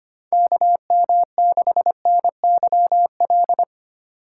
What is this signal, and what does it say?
Morse code, heard as a keyed tone, about 25 wpm, 700 Hz. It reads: KM6NYL
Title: USA Call Signs - 6 Characters - 8x Character Spacing - 25wpm